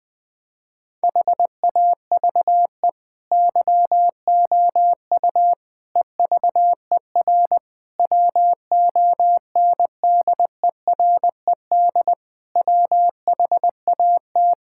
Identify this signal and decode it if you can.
Morse code, heard as a keyed tone, about 20 wpm, 700 Hz. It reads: HAVE YOU EVER WONDERED WHAT
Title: Difficult Sentences with Hints - 20wpm